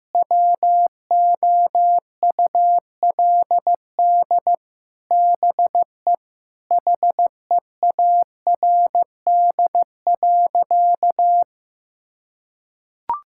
Morse code, heard as a keyed tone, about 16 wpm, 700 Hz. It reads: WOULD BE HEARD.
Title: Sentences from Top 1600 Words 15wpm